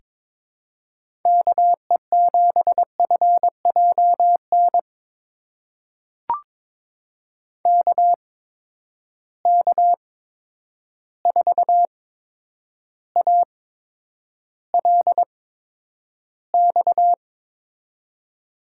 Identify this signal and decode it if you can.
Morse code, heard as a keyed tone, about 22 wpm, spaced as for 9 wpm, 700 Hz. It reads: KE7FJN KK4ALX
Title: USA Call Signs - 6 Characters - 8x Character Spacing - 22wpm